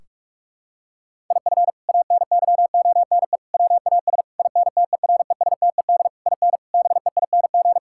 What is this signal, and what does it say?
Morse code, heard as a keyed tone, about 41 wpm, 700 Hz. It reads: IF ANYONE WAS INTERESTED IN BEING
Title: Sentences from Top 1600 Words 45wpm